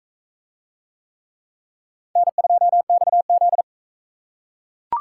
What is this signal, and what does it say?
Morse code, heard as a keyed tone, about 42 wpm, 700 Hz. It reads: N2XZ